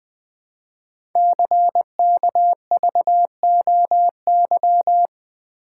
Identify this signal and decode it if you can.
Morse code, heard as a keyed tone, about 20 wpm, 700 Hz. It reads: CKVOY
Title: Code Groups - 5 Letters - 20wpm